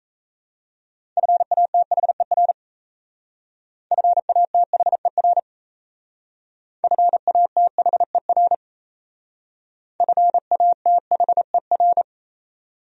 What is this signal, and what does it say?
Morse code, heard as a keyed tone, about 34 wpm, 700 Hz. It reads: FATHER FATHER FATHER FATHER
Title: Speed-Racing Top 300 Words 28wpm